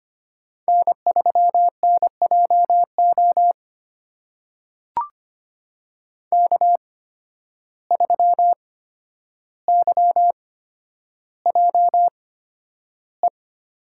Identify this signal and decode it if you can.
Morse code, heard as a keyed tone, about 25 wpm, spaced as for 11 wpm, 700 Hz. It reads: N3NJO K3YJE
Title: USA Call Signs - 5 Characters - 8x Character Spacing - 25wpm